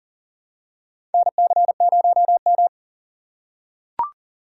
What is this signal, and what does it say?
Morse code, heard as a keyed tone, about 40 wpm, 700 Hz. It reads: NC0M
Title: USA Call Signs - 4 Characters - 4x Character Spacing - 40wpm